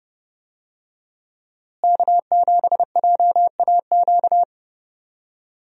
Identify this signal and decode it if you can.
Morse code, heard as a keyed tone, about 32 wpm, 700 Hz. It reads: K7JAQ
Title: USA Call Signs - 5 Characters - Repeated 2x - 30wpm